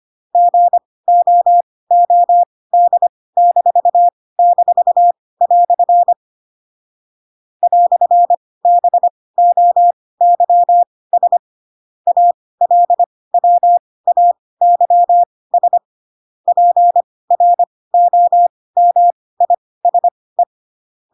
Morse code, heard as a keyed tone, about 25 wpm, spaced as for 18 wpm, 700 Hz. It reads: GOOD--" "BOYS ALWAYS PROMISE